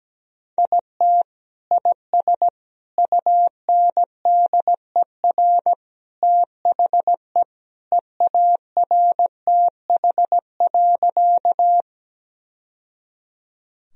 Morse code, heard as a keyed tone, about 17 wpm, 700 Hz. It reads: IT IS UNDER THE EARTH.
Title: Sentences from Top 1000 Words - Review Part 1 - 17wpm